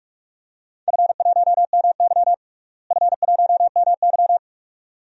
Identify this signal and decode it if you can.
Morse code, heard as a keyed tone, about 42 wpm, 700 Hz. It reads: F1MY F1MY
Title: DXCC-100 Call Signs - 4 Characters - Repeated 2x - 45wpm